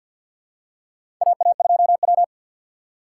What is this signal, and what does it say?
Morse code, heard as a keyed tone, about 49 wpm, 700 Hz. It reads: AA2W
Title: USA Call Signs - 4 Characters - 2x Character Spacing - 50wpm